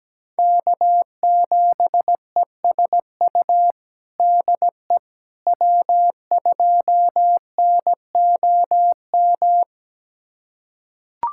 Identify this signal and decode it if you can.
Morse code, heard as a keyed tone, about 17 wpm, 700 Hz. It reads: K7ESU DE W2NOM
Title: USA Callsign DE Callsign - 5 Characters - 17wpm